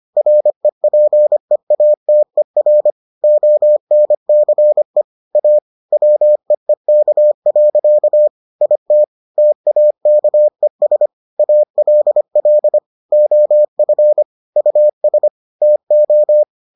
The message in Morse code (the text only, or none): REPEATER ONCE A WEEK. IT TAKES ALL OF US TO